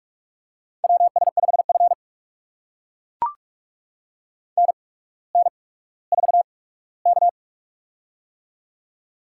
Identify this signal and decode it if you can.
Morse code, heard as a keyed tone, about 44 wpm, spaced as for 19 wpm, 700 Hz. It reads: WS5F NN4K
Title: USA Call Signs - 4 Characters - 8x Character Spacing - 45wpm